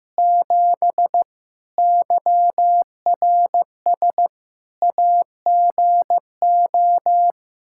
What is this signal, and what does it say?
Morse code, heard as a keyed tone, about 15 wpm, 700 Hz. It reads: 7 YRS AGO
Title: Practice QSO - Exchange 3 - More Info - 15wpm